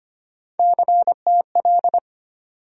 Morse code, heard as a keyed tone, about 25 wpm, 700 Hz. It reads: CTL
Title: USA Call Sign Postfixes - 3 Letters 25wpm